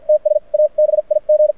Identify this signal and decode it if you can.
Morse code, heard as a keyed tone, about 53 wpm, spaced as for 44 wpm, 610 Hz. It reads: TS A BIG
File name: archived / CRSnet